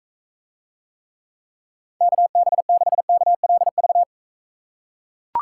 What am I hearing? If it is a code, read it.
Morse code, heard as a keyed tone, about 42 wpm, 700 Hz. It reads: KB6KLV